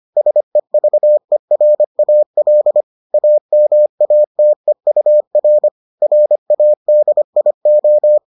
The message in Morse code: SEVERAL AMATEUR RADIO

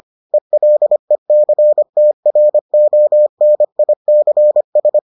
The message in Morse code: ELECTRONICS

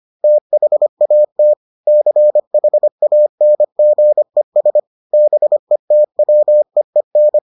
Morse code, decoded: THAT CHANGES BETWEEN

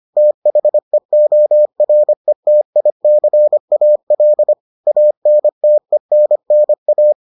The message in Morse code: THEORETICAL ANTENNA

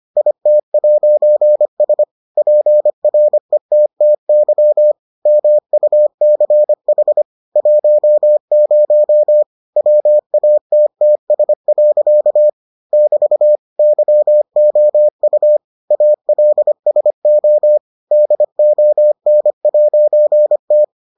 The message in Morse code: IT'S PRETTY MUCH 10 WATTS. <BT> YOU ALSO DON'T

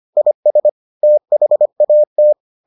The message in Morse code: IS THAT

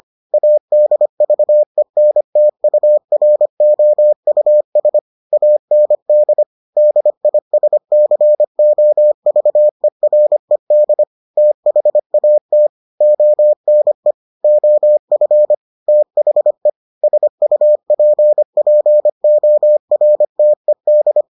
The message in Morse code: ADVENTUROUS AND DISCOVERED THAT ONE OF THE SUPPORTED

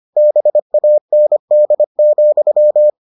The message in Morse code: BAND,

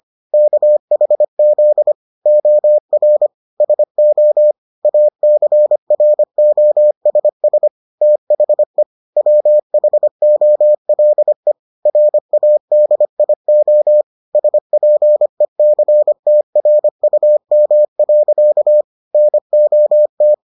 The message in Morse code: KHZ OR SO ACROSS THE WHOLE RADIO SPECTRUM. NOT

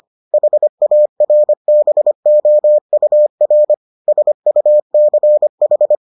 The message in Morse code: HARBOUR SUCH

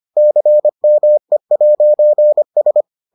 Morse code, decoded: CME'S

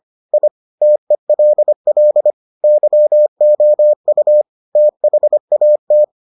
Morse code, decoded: I TELL YOU THAT

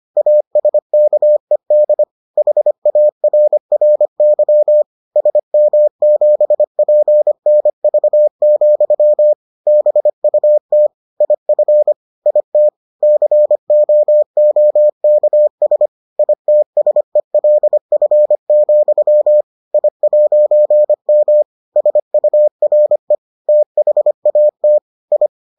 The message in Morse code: ASKED HARRY SM7PNV, BUT IF IT COOKS ITSELF, I'M SURE THAT I

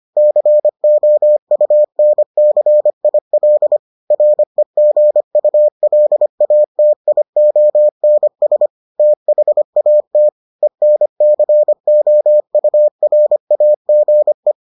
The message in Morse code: COUNCIL REGULATIONS THAT ENCOURAGE

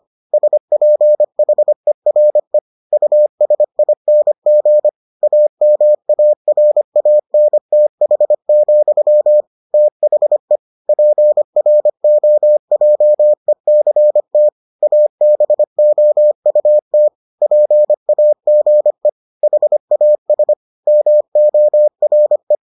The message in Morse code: SPHERE USING AMARANTH, THE PROJECT ABOUT PAGE HAS MORE

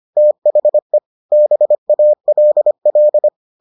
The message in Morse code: THE BALL

